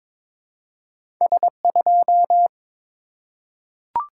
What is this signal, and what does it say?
Morse code, heard as a keyed tone, about 23 wpm, 700 Hz. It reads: S2
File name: practice / 22wpm / Two Letter-Number Combinations - 22wpm